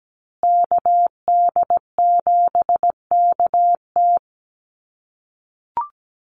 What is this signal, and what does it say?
Morse code, heard as a keyed tone, about 17 wpm, 700 Hz. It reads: KD7KT
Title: USA Call Signs Repeated 2x 17wpm